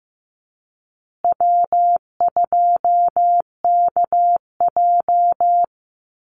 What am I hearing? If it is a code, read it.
Morse code, heard as a keyed tone, about 15 wpm, 700 Hz. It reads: W2KJ
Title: Easy Call Signs Repeated 3x - 15wpm